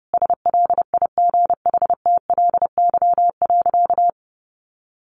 Morse code, heard as a keyed tone, about 30 wpm, 700 Hz. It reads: SLIGHTLY.
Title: Sentences from Top 1300 Words 30wpm